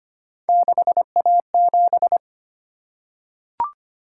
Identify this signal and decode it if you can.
Morse code, heard as a keyed tone, about 25 wpm, 700 Hz. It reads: 6A7
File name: Three Letter-Number Combinations - 25wpm